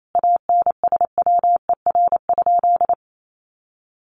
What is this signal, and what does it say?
Morse code, heard as a keyed tone, about 28 wpm, 700 Hz. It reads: ANSWER?
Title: Sentences from Top 300 Words 28wpm